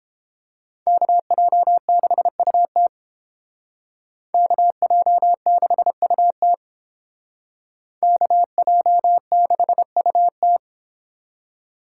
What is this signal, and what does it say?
Morse code, heard as a keyed tone, about 29 wpm, 700 Hz. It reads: KJ6UT KJ6UT KJ6UT